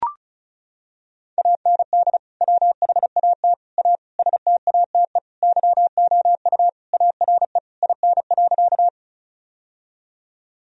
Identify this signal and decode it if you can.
Morse code, heard as a keyed tone, about 36 wpm, 700 Hz. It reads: AND WHAT A STATE YOU ARE IN.
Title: Sentences from Top 1000 Words - Review Part 2 - 35wpm